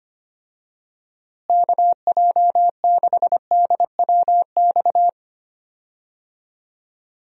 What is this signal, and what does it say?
Morse code, heard as a keyed tone, about 25 wpm, 700 Hz. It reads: KJ6DWX
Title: USA Call Signs - 6 Characters - Repeated 3x - 25wpm